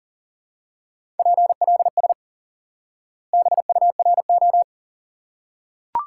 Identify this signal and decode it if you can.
Morse code, heard as a keyed tone, about 37 wpm, 700 Hz. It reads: PLS BURO
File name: IQE Course 3 - Lesson 89 - YAGI - Two Elements - 40wpm